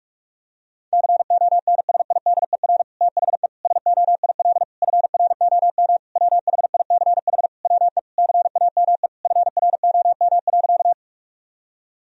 Morse code, heard as a keyed tone, about 46 wpm, 700 Hz. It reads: CONSIDER THE SOIL FROM WHICH WE CAME FROM.
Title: Sentences from Top 1000 Words - Review Part 4 - 45wpm